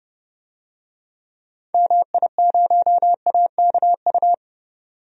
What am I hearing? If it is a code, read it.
Morse code, heard as a keyed tone, about 30 wpm, 700 Hz. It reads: MI0AKU
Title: DXCC-100 Call Signs - 6 Characters - 30wpm